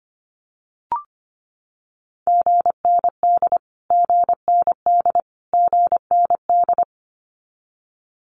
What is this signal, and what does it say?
Morse code, heard as a keyed tone, about 25 wpm, 700 Hz. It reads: GND GND GND
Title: Abbreviations Repeated 3x - 25wpm